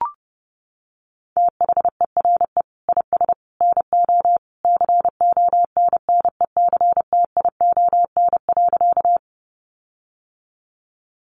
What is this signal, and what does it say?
Morse code, heard as a keyed tone, about 30 wpm, 700 Hz. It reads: THERE IS NO CONNECTION.
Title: Sentences from Top 1300 Words 30wpm